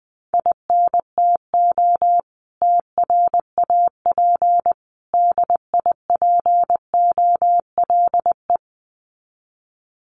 Morse code, heard as a keyed tone, about 19 wpm, 700 Hz. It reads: INTO TRAP DIPOLE
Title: Common QSO Phrases 20wpm